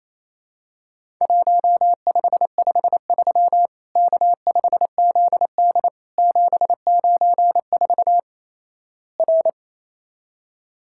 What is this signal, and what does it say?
Morse code, heard as a keyed tone, about 28 wpm, 700 Hz. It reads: 1553 K5ZD 794 R
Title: Worked All Europe DX Contest - QTC Exchange - 28wpm